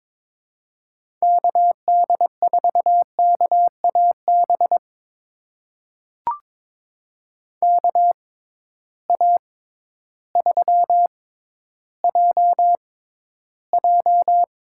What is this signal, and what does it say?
Morse code, heard as a keyed tone, about 21 wpm, spaced as for 12 wpm, 700 Hz. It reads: KD4KAB KA3JJ
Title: USA Call Signs - 6 Characters - 6x Character Spacing - 22wpm